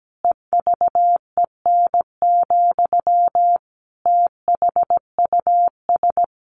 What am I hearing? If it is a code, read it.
Morse code, heard as a keyed tone, about 17 wpm, 700 Hz. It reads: EVEN, THUS
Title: A Sound Of Thunder 17wpm